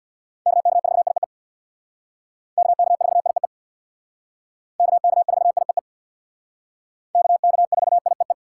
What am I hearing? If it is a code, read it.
Morse code, heard as a keyed tone, about 62 wpm, 700 Hz. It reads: KK4IEE KK4IEE KK4IEE KK4IEE
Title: Speed Racing - USA 2x3 Call Signs 50wpm